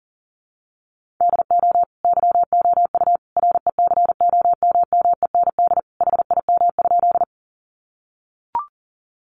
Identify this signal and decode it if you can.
Morse code, heard as a keyed tone, about 40 wpm, 700 Hz. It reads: DO YOU RECOMMEND HIM?
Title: Sentences from Top 1200 Words 40wpm